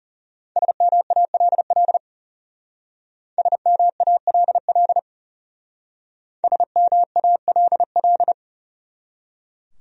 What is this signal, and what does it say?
Morse code, heard as a keyed tone, about 38 wpm, 700 Hz. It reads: SMALL SMALL SMALL